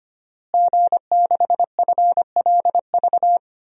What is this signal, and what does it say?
Morse code, heard as a keyed tone, about 25 wpm, 700 Hz. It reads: G6FLV
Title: DXCC-100 Call Signs - 5 Characters - Repeated 3x - 25wpm